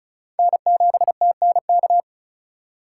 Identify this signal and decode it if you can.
Morse code, heard as a keyed tone, about 36 wpm, 700 Hz. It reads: N7TNK
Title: USA Call Signs - 5 Characters - 4x Character Spacing - 35wpm